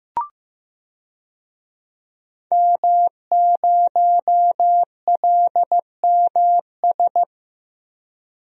Morse code, heard as a keyed tone, about 15 wpm, 700 Hz. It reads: M0LMS
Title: DXCC-100 Call Signs - 5 Characters - Repeated 2x - 15wpm